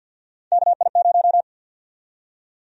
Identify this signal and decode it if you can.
Morse code, heard as a keyed tone, about 50 wpm, 700 Hz. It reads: KI0
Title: USA Call Sign Prefixes - 50wpm